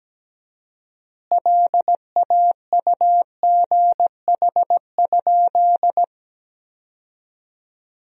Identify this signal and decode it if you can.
Morse code, heard as a keyed tone, about 17 wpm, 700 Hz. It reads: LAUGH?
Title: 4x Word-Spacing - Sentences from Top 500 Words - 17wpm